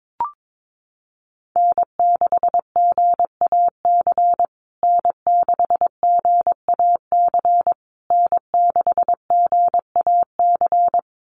Morse code, heard as a keyed tone, about 22 wpm, 700 Hz. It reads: N6GAC N6GAC N6GAC
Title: USA Call Signs Repeated 3x 22wpm